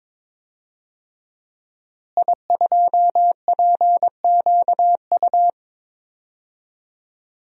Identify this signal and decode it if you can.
Morse code, heard as a keyed tone, about 22 wpm, 700 Hz. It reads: I2PQU
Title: DXCC-100 Call Signs - 5 Characters - Repeated 3x - 22wpm